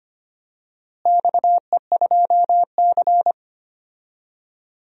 Morse code, heard as a keyed tone, about 25 wpm, 700 Hz. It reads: XE2C
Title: DXCC-100 Call Signs - 4 Characters - Repeated 3x - 25wpm